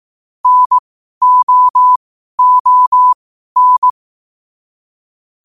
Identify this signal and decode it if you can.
Morse code, heard as a keyed tone, about 18 wpm, spaced as for 12 wpm, 1000 Hz. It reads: NOON
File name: qotdfiles / 12wpm